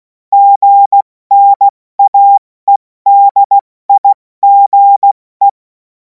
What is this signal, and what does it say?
Morse code, heard as a keyed tone, about 16 wpm, 800 Hz. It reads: GNAEDIGE
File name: Morsekurs / Witze / Witz 24. März 2026 70 F80